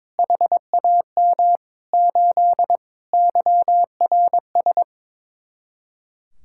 Morse code, heard as a keyed tone, about 22 wpm, 700 Hz. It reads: HAM 8 YRS
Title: Speed-Racing Common QSO Phrases Encore - 22wpm